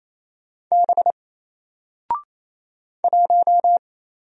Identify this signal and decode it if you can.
Morse code, heard as a keyed tone, about 28 wpm, 700 Hz. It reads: B 1
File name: Single-Letter-Number-28wpm